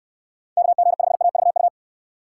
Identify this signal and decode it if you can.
Morse code, heard as a keyed tone, about 69 wpm, 700 Hz. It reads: KG5AVU